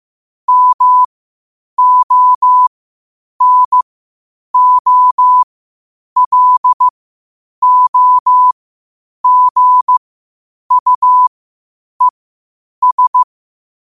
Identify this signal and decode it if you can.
Morse code, heard as a keyed tone, about 15 wpm, spaced as for 8 wpm, 1000 Hz. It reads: MONOLOGUES
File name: qotdfiles / 8wpm